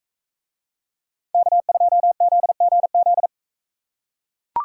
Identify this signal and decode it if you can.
Morse code, heard as a keyed tone, about 41 wpm, 700 Hz. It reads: K2ZGZ